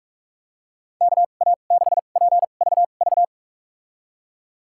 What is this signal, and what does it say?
Morse code, heard as a keyed tone, about 46 wpm, spaced as for 32 wpm, 700 Hz. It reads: KA6PVV
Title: USA Call Signs - 6 Characters - 2x Character Spacing - 45wpm